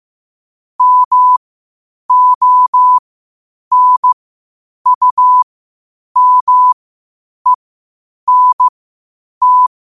Morse code, heard as a keyed tone, about 15 wpm, spaced as for 8 wpm, 1000 Hz. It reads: MONUMENT